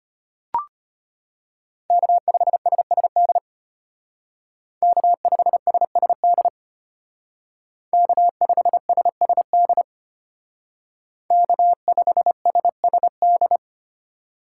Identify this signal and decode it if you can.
Morse code, heard as a keyed tone, about 30 wpm, 700 Hz. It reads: K5SSD K5SSD K5SSD K5SSD
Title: Speed Racing - USA 1x3 Call Signs 25wpm